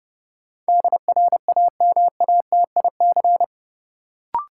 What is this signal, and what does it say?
Morse code, heard as a keyed tone, about 30 wpm, 700 Hz. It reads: DRAMATIC
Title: Sentences from Top 1300 Words - New Word Drill - 30wpm